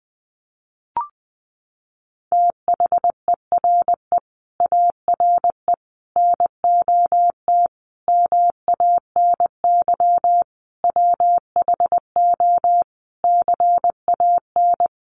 Morse code, heard as a keyed tone, about 19 wpm, 700 Hz. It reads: THERE ARE NOT MANY WHO CAN
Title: Sentences from Top 1300 Words 20wpm